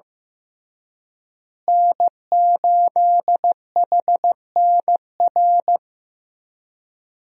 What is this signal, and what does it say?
Morse code, heard as a keyed tone, about 15 wpm, 700 Hz. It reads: N8HNR
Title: USA Call Signs - 5 Characters - Repeated 3x - 15wpm